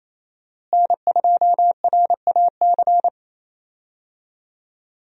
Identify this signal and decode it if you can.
Morse code, heard as a keyed tone, about 28 wpm, 700 Hz. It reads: N2RAC
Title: USA Call Signs - 5 Characters - Repeated 3x - 28wpm